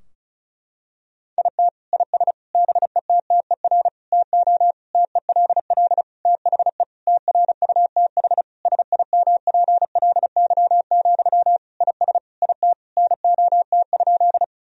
Morse code, heard as a keyed tone, about 34 wpm, 700 Hz. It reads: IT IS BETTER TO TELL THE TRUTH SIMPLY, IS IT NOT?
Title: Sentences from Top 1000 Words - Review Part 4 - 35wpm